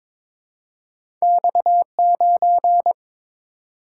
Morse code, heard as a keyed tone, about 22 wpm, 700 Hz. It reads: X9
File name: practice / 22wpm / Two Letter-Number Combinations - 22wpm